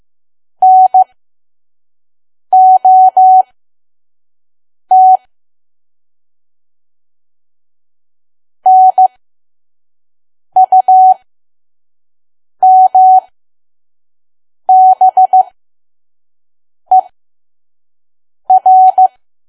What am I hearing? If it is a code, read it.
Morse code, heard as a keyed tone, about 15 wpm, spaced as for 5 wpm, 750 Hz. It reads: NOT NUMBER
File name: Archive / 5 WPM files / PM